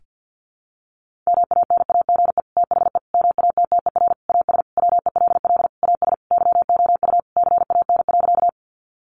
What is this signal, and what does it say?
Morse code, heard as a keyed tone, about 50 wpm, 700 Hz. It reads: MANAGE THE MATTER AS WELL AS YOU CAN.
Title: Sentences from Top 1400 Words 50wpm